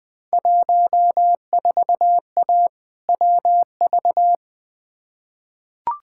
Morse code, heard as a keyed tone, about 20 wpm, 700 Hz. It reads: 14A WV
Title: Field Day Exchanges Encore - 20wpm